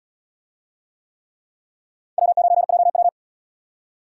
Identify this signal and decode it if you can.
Morse code, heard as a keyed tone, about 72 wpm, 700 Hz. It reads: W9JW